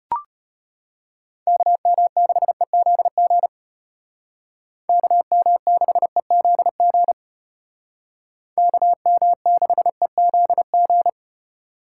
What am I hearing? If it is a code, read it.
Morse code, heard as a keyed tone, about 33 wpm, 700 Hz. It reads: KM6EZG KM6EZG KM6EZG